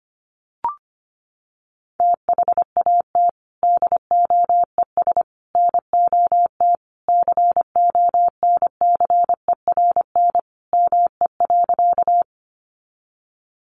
Morse code, heard as a keyed tone, about 25 wpm, 700 Hz. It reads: THAT DOES NOT CONCERN ME.
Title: Sentences from Top 1000 Words 25wpm